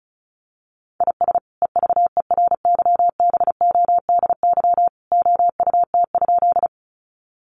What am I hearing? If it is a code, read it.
Morse code, heard as a keyed tone, about 34 wpm, 700 Hz. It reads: IS EVERYBODY OUT?
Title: Sentences from Top 1400 Words 35wpm